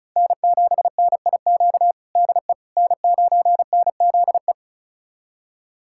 Morse code, heard as a keyed tone, about 34 wpm, 700 Hz. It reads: N7NIQ DE N9NZE
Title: USA Callsign DE Callsign - 5 Characters - 35wpm